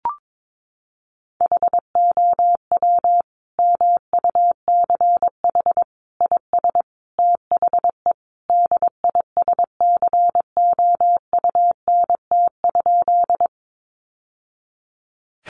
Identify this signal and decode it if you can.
Morse code, heard as a keyed tone, about 22 wpm, 700 Hz. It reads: HOW MUCH IS THE DISCOUNT?
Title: Sentences from Top 1200 Words 22wpm